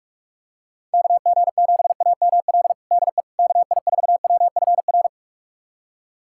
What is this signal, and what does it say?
Morse code, heard as a keyed tone, about 43 wpm, 690 Hz. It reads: KG7AML DE KI4WFR
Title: USA Callsign DE Callsign 45wpm